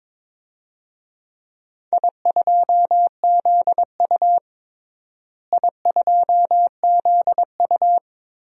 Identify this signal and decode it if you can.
Morse code, heard as a keyed tone, about 22 wpm, 700 Hz. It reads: I2ZU I2ZU
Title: DXCC-100 Call Signs - 4 Characters - Repeated 3x - 22wpm